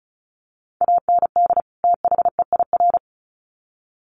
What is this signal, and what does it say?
Morse code, heard as a keyed tone, about 34 wpm, 700 Hz. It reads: AND THEIR
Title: Sets of 2 Words 35wpm